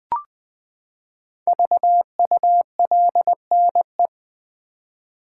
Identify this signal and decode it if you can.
Morse code, heard as a keyed tone, about 20 wpm, 700 Hz. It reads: VULNE
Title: Build A Word - 5 Letters At A Time - 10-Letter Words - 20wpm